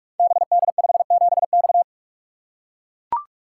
Morse code, heard as a keyed tone, about 44 wpm, 700 Hz. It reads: BD57X
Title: Code Groups - 5 Letter:Numbers - 45wpm